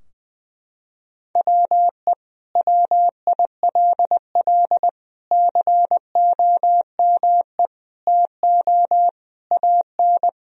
Morse code, heard as a keyed tone, about 20 wpm, 700 Hz. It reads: WE WILL COME TO AN
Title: Sentences from Top 1000 Words - Review Part 3 - 20wpm